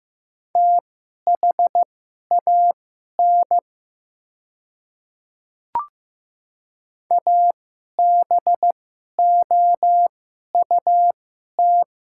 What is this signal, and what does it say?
Morse code, heard as a keyed tone, about 15 wpm, spaced as for 11 wpm, 700 Hz. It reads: THAN ABOUT
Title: 2x Character Spacing - Top 100 Words - 15wpm